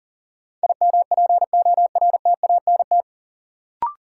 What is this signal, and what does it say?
Morse code, heard as a keyed tone, about 40 wpm, 700 Hz. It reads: IMPORTANT
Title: Sets of 3 Words - New Word Drill - 40wpm